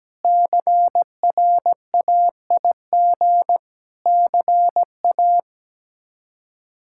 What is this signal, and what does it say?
Morse code, heard as a keyed tone, about 17 wpm, 700 Hz. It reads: CRAIG CA
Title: K1USN SST QSO Practice - 17wpm